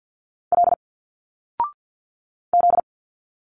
Morse code, heard as a keyed tone, about 50 wpm, 700 Hz. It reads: L Z
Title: Single Letters 50wpm